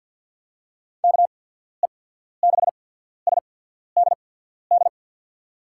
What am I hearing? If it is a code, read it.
Morse code, heard as a keyed tone, about 50 wpm, spaced as for 14 wpm, 700 Hz. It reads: KE6SDD
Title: USA Call Signs - 6 Characters - 8x Character Spacing - 50wpm